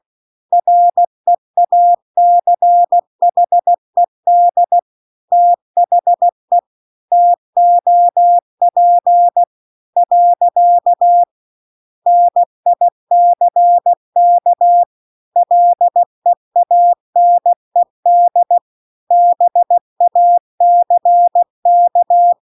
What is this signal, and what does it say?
Morse code, heard as a keyed tone, about 16 wpm, 700 Hz. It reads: REACHED THE TOP . NICK LEANED BACK